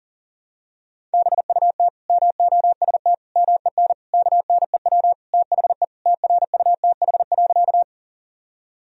Morse code, heard as a keyed tone, about 37 wpm, 700 Hz. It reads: BUT MOST MEN KNEW THE TRUTH.
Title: Sentences from Top 1000 Words - Review Part 1 - 40wpm